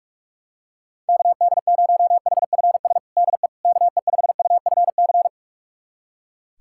Morse code, heard as a keyed tone, about 42 wpm, 700 Hz. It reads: KD0HFS DE KE5UFC
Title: USA Callsign DE Callsign - 6 Characters - 45wpm